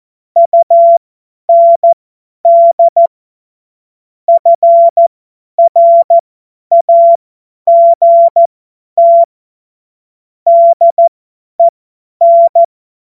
Morse code, heard as a keyed tone, about 14 wpm, spaced as for 10 wpm, 680 Hz. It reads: UND FRAGT DEN
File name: Morsekurs / Witze / Witz 04. Oktober 2022 50 F70